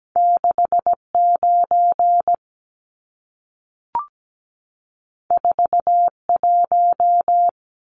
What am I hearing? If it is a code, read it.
Morse code, heard as a keyed tone, about 17 wpm, 700 Hz. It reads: 69 41
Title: Numbers - 2-Digit 17wpm